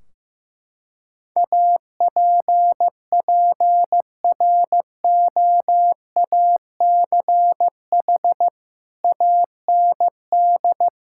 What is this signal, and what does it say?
Morse code, heard as a keyed tone, about 15 wpm, 700 Hz. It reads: APPROACH AND
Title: Sentences from Top 1600 Words 15wpm